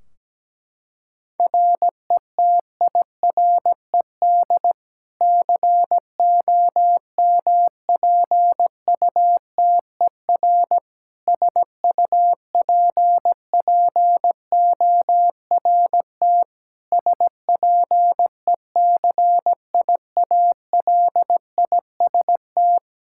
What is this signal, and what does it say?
Morse code, heard as a keyed tone, about 17 wpm, 700 Hz. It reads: RETIRED COMPUTER SUPPORT SPECIALIST
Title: Practice QSO - Exchange 3 - More Info - 17wpm